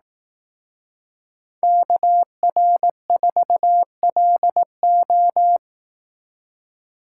Morse code, heard as a keyed tone, about 18 wpm, 700 Hz. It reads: KR4LO
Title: Speed Racing - USA 2x2 Call Signs 15wpm